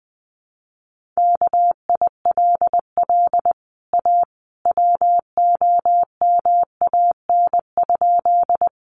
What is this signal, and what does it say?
Morse code, heard as a keyed tone, about 20 wpm, 700 Hz. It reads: KILL A WOMAN?
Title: Sentences from Top 1400 Words 20wpm